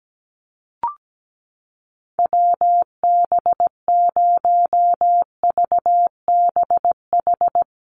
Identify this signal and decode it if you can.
Morse code, heard as a keyed tone, about 17 wpm, 700 Hz. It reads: WB0VBH
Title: USA Call Signs Repeated 3x 17wpm